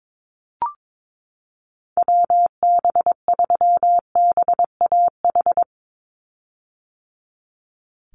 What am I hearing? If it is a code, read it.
Morse code, heard as a keyed tone, about 22 wpm, 700 Hz. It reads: WB3BAH
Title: USA Call Signs 22wpm